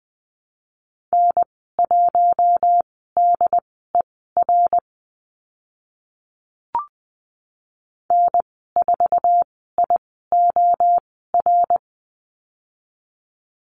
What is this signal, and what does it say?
Morse code, heard as a keyed tone, about 20 wpm, spaced as for 15 wpm, 700 Hz. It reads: N1DER N4IOR
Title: 069 Lesson 12g - Letter d - Call Signs 20wpm